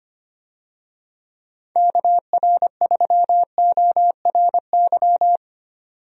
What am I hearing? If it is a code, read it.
Morse code, heard as a keyed tone, about 25 wpm, 700 Hz. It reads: KR3ORY
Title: USA Call Signs - 6 Characters - 25wpm